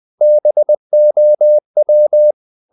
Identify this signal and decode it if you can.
Morse code, heard as a keyed tone, about 20 wpm, 600 Hz. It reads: BOW
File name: the old man and the sea part 2 20wpm